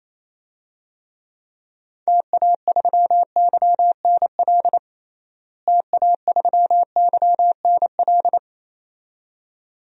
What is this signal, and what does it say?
Morse code, heard as a keyed tone, about 28 wpm, 700 Hz. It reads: TA3YNL TA3YNL
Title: DXCC-100 Call Signs - 6 Characters - Repeated 2x - 28wpm